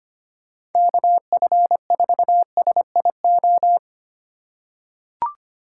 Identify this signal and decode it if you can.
Morse code, heard as a keyed tone, about 25 wpm, 700 Hz. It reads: KF4SIO
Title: USA Call Signs - 3x Character Spacing - 25wpm